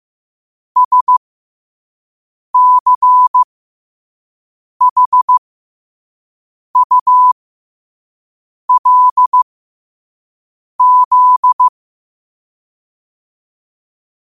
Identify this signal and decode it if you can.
Morse code, heard as a keyed tone, about 15 wpm, spaced as for 5 wpm, 1000 Hz. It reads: SCHULZ
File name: qotdfiles / 5wpm